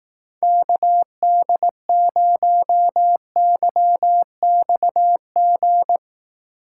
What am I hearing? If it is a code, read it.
Morse code, heard as a keyed tone, about 18 wpm, 700 Hz. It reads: KD0YXG
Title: Speed Racing - USA 2x3 Call Signs 15wpm